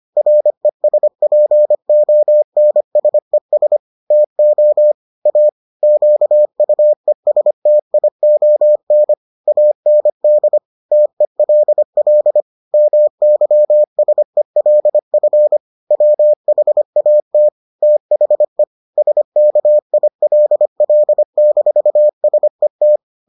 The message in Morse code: RESPONSES TO A QUESTION AND TELL MYSELF WHAT THE SKILL-SET

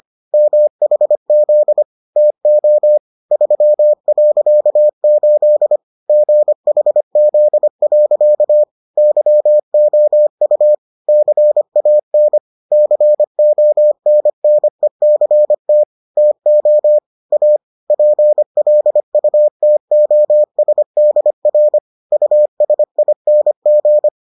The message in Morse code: MHZ TO 3.8 GHZ. YOU CAN CONNECT TO A PLUTOSDR USING

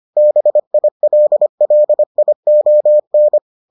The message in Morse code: BILLION

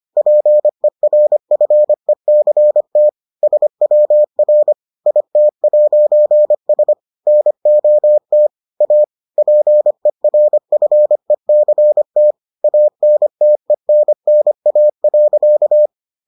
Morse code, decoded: PERFECT SWR IT'S NOT A PERFECT ANTENNA.

